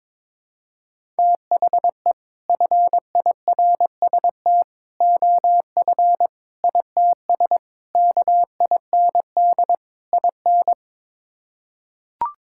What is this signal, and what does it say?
Morse code, heard as a keyed tone, about 21 wpm, 700 Hz. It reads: THE FIRST OF ITS KIND IN
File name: practice / 22wpm / Sets of 6 Words - 22wpm